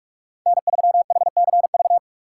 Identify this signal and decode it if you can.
Morse code, heard as a keyed tone, about 46 wpm, 700 Hz. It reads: N3HCV
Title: USA Call Signs - 5 Characters - 2x Character Spacing - 45wpm